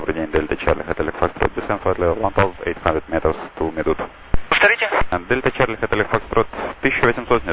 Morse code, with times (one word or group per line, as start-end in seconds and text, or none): none